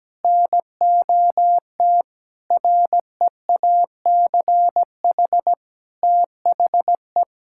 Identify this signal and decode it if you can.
Morse code, heard as a keyed tone, about 17 wpm, 700 Hz. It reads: NOT REACH THE
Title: Sentences from Top 1000 Words - Review Part 6 - 17wpm